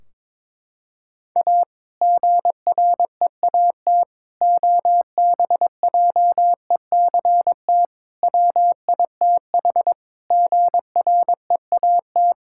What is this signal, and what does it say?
Morse code, heard as a keyed tone, about 22 wpm, 700 Hz. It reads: A GREAT OBJECT WITH GREAT
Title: Sentences from Top 1600 Words 22wpm